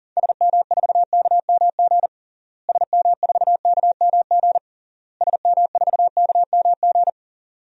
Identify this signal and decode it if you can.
Morse code, heard as a keyed tone, about 40 wpm, 700 Hz. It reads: SM4KMG SM4KMG SM4KMG
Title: DXCC-100 Call Signs - 6 Characters - Repeated 3x - 40wpm